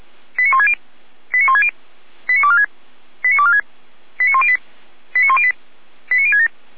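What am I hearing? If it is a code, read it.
Five-tone selective call (ZVEI), heard as quick runs of five tones, tones of about 70 ms. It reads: 89160 89160 89246 89246 89108 89108 89067
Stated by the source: ZVEI five-tone call received by SDR.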